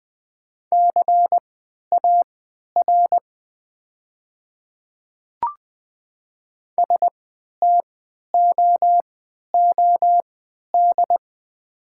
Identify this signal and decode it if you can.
Morse code, heard as a keyed tone, about 20 wpm, spaced as for 11 wpm, 700 Hz. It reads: CAR STOOD
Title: 3x Character Spacing - Top 500 Words - 20wpm